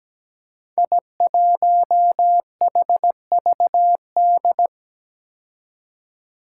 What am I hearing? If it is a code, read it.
Morse code, heard as a keyed tone, about 17 wpm, 700 Hz. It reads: I1HVD
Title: Five Challenging Letter-Number Combinations 17wpm